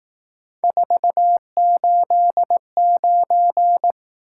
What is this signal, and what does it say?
Morse code, heard as a keyed tone, about 18 wpm, 700 Hz. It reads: 489
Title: Speed-Racing Top 180 QSO Elements 15wpm